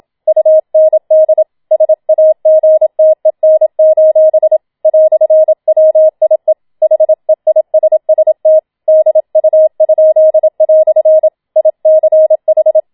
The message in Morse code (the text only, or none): UND SAGTEN: "WIE HEISST DU?" ICH